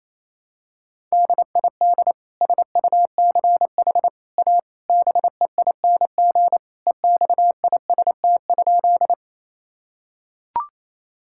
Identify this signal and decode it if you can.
Morse code, heard as a keyed tone, about 27 wpm, 700 Hz. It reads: DID SUCH A BEING EXIST?
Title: Sentences from Top 1000 Words - Review Part 5 - 28wpm